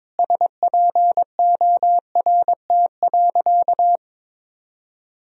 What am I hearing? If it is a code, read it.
Morse code, heard as a keyed tone, about 22 wpm, 700 Hz. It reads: SPORT.
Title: Sentences from Top 1000 Words - Review Part 4 - 22wpm